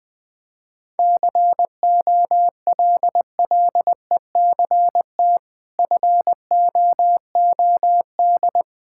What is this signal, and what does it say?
Morse code, heard as a keyed tone, about 21 wpm, 700 Hz. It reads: COLLECT FOOD
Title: Sentences from Top 1600 Words 20wpm